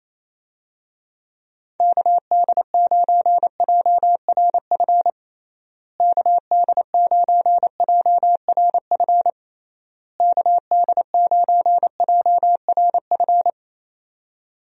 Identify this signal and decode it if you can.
Morse code, heard as a keyed tone, about 28 wpm, 700 Hz. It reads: KD9JRF KD9JRF KD9JRF
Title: USA Call Signs - 6 Characters - Repeated 3x - 28wpm